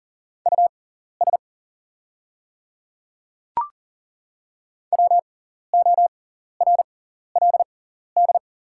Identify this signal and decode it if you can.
Morse code, heard as a keyed tone, about 40 wpm, spaced as for 14 wpm, 700 Hz. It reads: US WORLD
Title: 6x Character Spacing - Top 200 Words - 40wpm